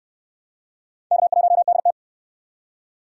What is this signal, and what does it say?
Morse code, heard as a keyed tone, about 60 wpm, 700 Hz. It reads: K1DT